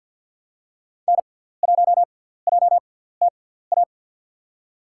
Morse code, heard as a keyed tone, about 52 wpm, spaced as for 17 wpm, 700 Hz. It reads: N1JTA
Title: USA Call Signs - 6x Character Spacing - 50wpm